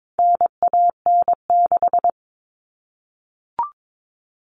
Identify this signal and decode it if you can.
Morse code, heard as a keyed tone, about 22 wpm, 700 Hz. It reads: NAN6
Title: All Cut Numbers - 4-Digit 22wpm